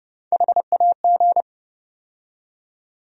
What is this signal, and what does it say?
Morse code, heard as a keyed tone, about 30 wpm, 700 Hz. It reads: HAG
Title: USA Call Sign Postfixes - 3 Letters 30wpm